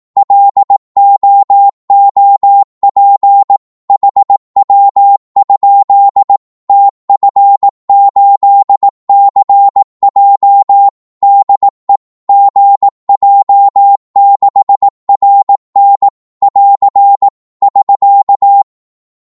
Morse code, heard as a keyed tone, about 18 wpm, 800 Hz. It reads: LOOP HW? TF8CJ DE GJ6RN <AR> <SK>
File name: Morse / qso-18wpm